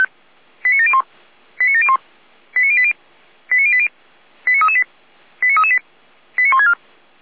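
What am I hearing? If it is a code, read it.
Five-tone selective call (ZVEI), heard as quick runs of five tones, tones of about 70 ms. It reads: 89701 89701 89080 89080 89338 89338 89164
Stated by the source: ZVEI five-tone selective call received on an SDR.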